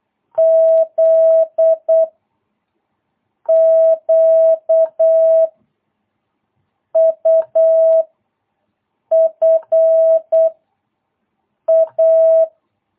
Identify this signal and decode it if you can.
Morse code, heard as a keyed tone, about 8 wpm, 650 Hz. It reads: Z Q U F A
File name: jh8cbh / 2022cw